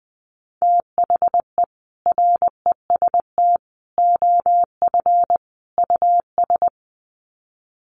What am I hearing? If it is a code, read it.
Morse code, heard as a keyed tone, about 19 wpm, 700 Hz. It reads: THE REST OF US
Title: Sets of 4 Words 20wpm